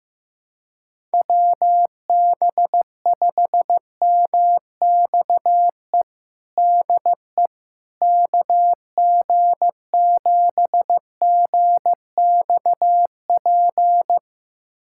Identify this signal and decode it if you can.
Morse code, heard as a keyed tone, about 15 wpm, 700 Hz. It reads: WB5MXE DE KG7GXP
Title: USA Callsign DE Callsign - 6 Characters - 15wpm